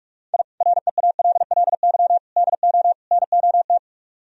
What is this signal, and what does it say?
Morse code, heard as a keyed tone, about 59 wpm, spaced as for 52 wpm, 700 Hz. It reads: I REALLY DO NOT
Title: Speed-Racing - Sets of 4 Words 45wpm